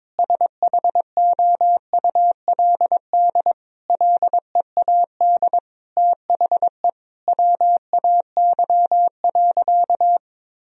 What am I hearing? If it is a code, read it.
Morse code, heard as a keyed tone, about 21 wpm, 700 Hz. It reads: SHOULD LEAD THE WAY.
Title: Sentences from Top 1000 Words - Review Part 2 - 22wpm